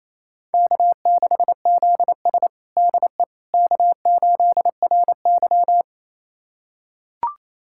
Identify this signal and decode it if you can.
Morse code, heard as a keyed tone, about 28 wpm, 700 Hz. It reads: K6ZS DE K8RY
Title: USA Callsign DE Callsign - 4 Characters - 28wpm